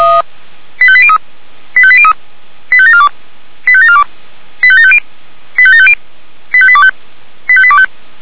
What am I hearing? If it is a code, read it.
Five-tone selective call (ZVEI), heard as quick runs of five tones, tones of about 70 ms. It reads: 85803 85803 85742 85742 85750 85750 85825 85825